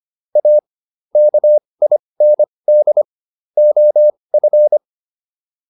Morse code, heard as a keyed tone, about 25 wpm, spaced as for 20 wpm, 600 Hz. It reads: A KIND OF